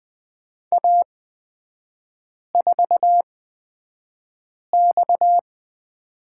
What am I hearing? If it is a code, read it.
Morse code, heard as a keyed tone, about 20 wpm, spaced as for 5 wpm, 690 Hz. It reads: A4X
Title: 229 Lesson 37e - Letter x - Three Characters 20wpm